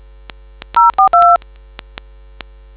DTMF, heard as telephone touch tones, tones of about 130 ms. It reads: 042